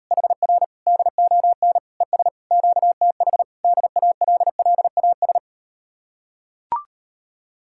Morse code, heard as a keyed tone, about 38 wpm, 700 Hz. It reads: HR DON ES QTH DALLAS